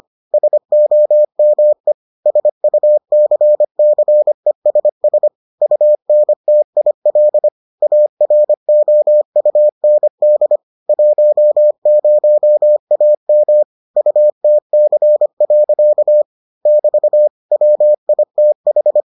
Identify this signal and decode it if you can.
Morse code, heard as a keyed tone, about 25 wpm, 600 Hz. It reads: SOME SUCCESS UNTIL AROUND 10AM UTC. <BT> WITH